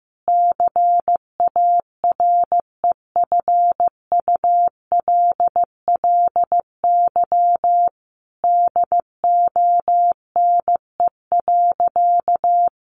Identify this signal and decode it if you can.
Morse code, heard as a keyed tone, about 15 wpm, 700 Hz. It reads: CAREFULLY DONE.
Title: Sentences from Top 1300 Words 15wpm